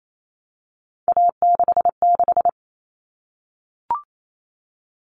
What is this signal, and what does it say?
Morse code, heard as a keyed tone, about 28 wpm, 700 Hz. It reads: A66
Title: All Cut Numbers - 3-Digit 28wpm